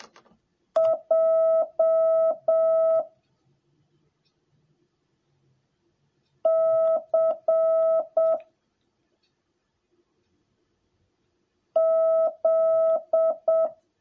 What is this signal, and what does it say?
Morse code, heard as a keyed tone, about 7 wpm, 650 Hz. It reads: J C Z
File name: jh8cbh / 2021cw / cw10a